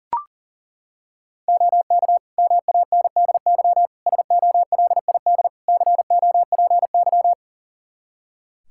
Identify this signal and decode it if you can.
Morse code, heard as a keyed tone, about 40 wpm, 700 Hz. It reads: OK MANDY SOLID COPY
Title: Practice QSO - Exchange 3 - More Info - 40wpm